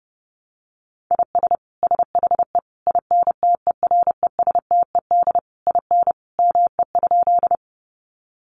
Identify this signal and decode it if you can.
Morse code, heard as a keyed tone, about 30 wpm, 700 Hz. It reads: IS SHE INTERESTED IN ME?
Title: Sentences from Top 900 Words 30wpm